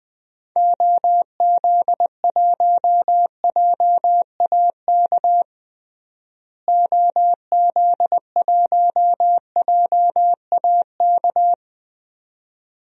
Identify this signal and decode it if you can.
Morse code, heard as a keyed tone, about 21 wpm, 700 Hz. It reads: OZ1JAK OZ1JAK
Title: DXCC-100 Call Signs - 6 Characters - Repeated 2x - 20wpm